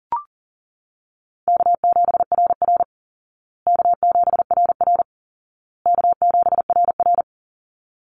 Morse code, heard as a keyed tone, about 40 wpm, 700 Hz. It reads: K7RR K7RR K7RR
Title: Easy Call Signs Repeated 3x - 40wpm